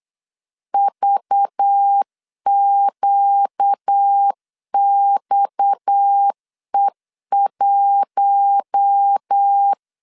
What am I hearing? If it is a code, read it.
Morse code, heard as a keyed tone, about 8 wpm, 800 Hz. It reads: VQXE1
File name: LEZIONI / 10 SIGNS / 10A SLASH 8wpm 3452